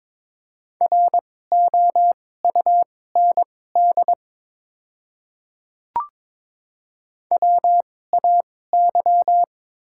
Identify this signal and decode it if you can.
Morse code, heard as a keyed tone, about 22 wpm, spaced as for 16 wpm, 700 Hz. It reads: ROUND WAY
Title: 2x Character Spacing - Top 200 Words - 22wpm